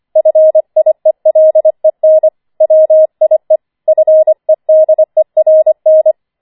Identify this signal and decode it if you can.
Morse code, heard as a keyed tone, about 25 wpm, 630 Hz. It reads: FIELEN WIE FEDERN